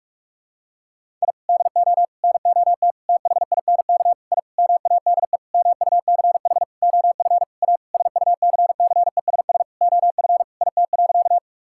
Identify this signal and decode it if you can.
Morse code, heard as a keyed tone, about 46 wpm, 700 Hz. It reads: I DO NOT THINK I MADE MUCH OF A SUCCESS OF IT.
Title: Sentences from Top 1000 Words - Review Part 6 - 45wpm